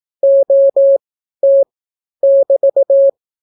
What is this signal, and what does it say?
Morse code, heard as a keyed tone, about 18 wpm, 550 Hz. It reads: O T <BT>